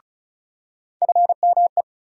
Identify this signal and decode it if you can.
Morse code, heard as a keyed tone, about 36 wpm, 700 Hz. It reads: FME
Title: USA Call Sign Postfixes - 3 Letters 35wpm